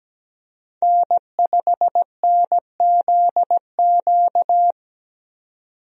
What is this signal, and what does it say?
Morse code, heard as a keyed tone, about 17 wpm, 700 Hz. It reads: N5NZQ
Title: USA Call Signs - 5 Characters - 17wpm